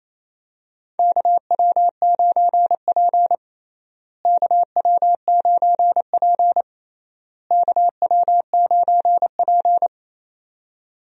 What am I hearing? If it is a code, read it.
Morse code, heard as a keyed tone, about 28 wpm, 700 Hz. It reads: KW9P KW9P KW9P
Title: USA Call Signs - 4 Characters - Repeated 3x - 28wpm